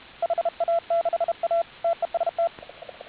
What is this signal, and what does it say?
Morse code, heard as a keyed tone, about 32 wpm, 690 Hz. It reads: HA6A TEST